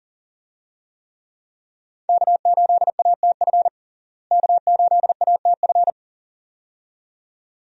Morse code, heard as a keyed tone, about 40 wpm, 700 Hz. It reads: K8ATF K8ATF
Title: USA Call Signs - 5 Characters - Repeated 2x - 40wpm